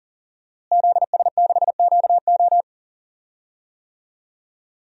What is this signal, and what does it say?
Morse code, heard as a keyed tone, about 40 wpm, 700 Hz. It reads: ZS6QO
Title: DXCC-100 Call Signs - 5 Characters - Repeated 3x - 40wpm